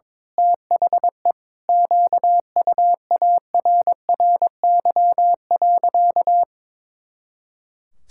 Morse code, heard as a keyed tone, about 22 wpm, 700 Hz. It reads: THE QUARRY.
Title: Pangram Challenge v2 - 22wpm